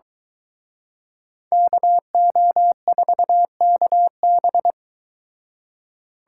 Morse code, heard as a keyed tone, about 24 wpm, 700 Hz. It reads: KO4KB